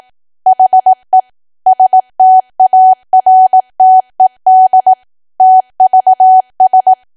Morse code, heard as a keyed tone, about 18 wpm, 750 Hz. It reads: HE STARTED TVS
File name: Archive / 18 WPM / 18WPM